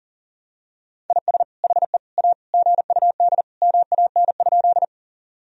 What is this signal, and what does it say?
Morse code, heard as a keyed tone, about 40 wpm, 700 Hz. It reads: IS HE A GUD MAN?
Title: Sentences from Top 200 Words and Abbreviations 40wpm